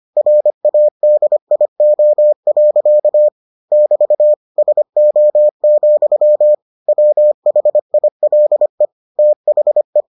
Morse code, heard as a keyed tone, about 25 wpm, 600 Hz. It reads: RADIO. <BT> SO, WHILE THE